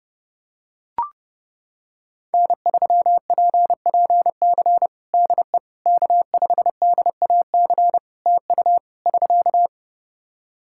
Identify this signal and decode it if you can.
Morse code, heard as a keyed tone, about 30 wpm, 700 Hz. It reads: N3PPC DE K5DAC TU <SK>
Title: K1USN SST QSO Practice 30wpm